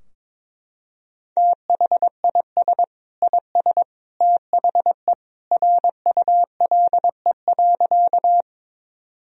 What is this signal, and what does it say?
Morse code, heard as a keyed tone, about 21 wpm, 700 Hz. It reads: THIS IS THE RULE.
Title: Sentences from Top 1000 Words - Review Part 2 - 22wpm